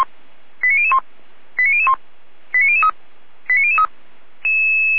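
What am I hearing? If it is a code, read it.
Five-tone selective call (ZVEI), heard as quick runs of five tones, tones of about 70 ms. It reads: 89001 89001 89003 89003
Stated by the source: ZVEI five-tone selective call received by SDR.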